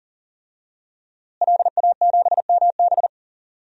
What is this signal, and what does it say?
Morse code, heard as a keyed tone, about 40 wpm, 700 Hz. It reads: LA7MB
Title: DXCC-100 Call Signs - 5 Characters - Repeated 2x - 40wpm